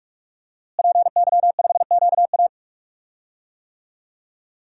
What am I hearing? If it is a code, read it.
Morse code, heard as a keyed tone, about 41 wpm, 700 Hz. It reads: PY5QA
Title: DXCC-100 Call Signs - 5 Characters - Repeated 3x - 45wpm